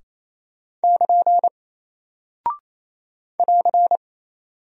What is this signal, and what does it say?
Morse code, heard as a keyed tone, about 28 wpm, 700 Hz. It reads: <KN> <AR>
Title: Single Character (Letter:Number:Punctuation:Prosign) Alt 28wpm